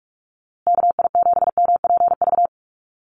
Morse code, heard as a keyed tone, about 44 wpm, 700 Hz. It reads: KI7MPV
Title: USA Call Signs Repeated 3x 45wpm